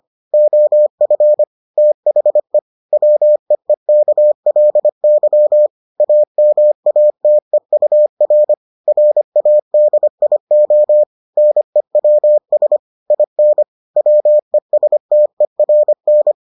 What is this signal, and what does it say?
Morse code, heard as a keyed tone, about 25 wpm, 600 Hz. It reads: OF THE WEEKLY AMATEUR RADIO NEWS IN WESTERN